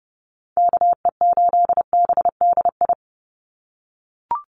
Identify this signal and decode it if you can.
Morse code, heard as a keyed tone, about 30 wpm, 700 Hz. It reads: KE8BDI
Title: USA Call Signs 30wpm